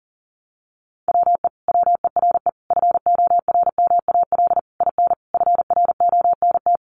Morse code, heard as a keyed tone, about 40 wpm, 700 Hz. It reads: WE WERE FORMAL IN FRONT
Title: Sentences from Top 1300 Words 40wpm